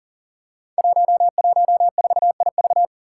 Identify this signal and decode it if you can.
Morse code, heard as a keyed tone, about 40 wpm, 700 Hz. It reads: 114IV
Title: Five Challenging Letter-Number Combinations 40wpm